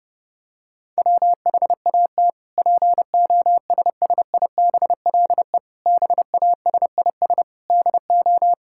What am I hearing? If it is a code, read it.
Morse code, heard as a keyed tone, about 30 wpm, 700 Hz. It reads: WHAT POSSIBLE BASIS DO
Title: Sentences from Top 1000 Words - Review Part 5 - 30wpm